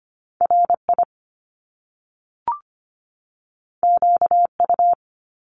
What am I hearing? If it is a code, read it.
Morse code, heard as a keyed tone, about 25 wpm, 700 Hz. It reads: R I Q U
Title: Top Two Letter Combinations in Words 25wpm